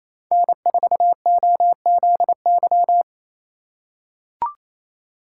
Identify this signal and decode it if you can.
Morse code, heard as a keyed tone, about 28 wpm, 700 Hz. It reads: N4OZY
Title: USA Call Signs - 5 Characters - 4x Character Spacing - 28wpm